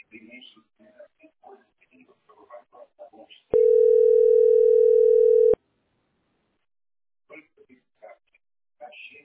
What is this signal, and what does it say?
Ringback tone (North America), 440 + 480 Hz, with speech.
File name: 212fca77-6083-4a88-8769